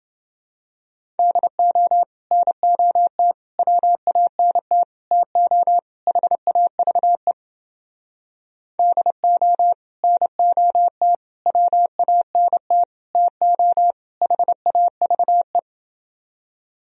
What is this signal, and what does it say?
Morse code, heard as a keyed tone, about 29 wpm, 700 Hz. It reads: DO NOT WANT TO HAVE DO NOT WANT TO HAVE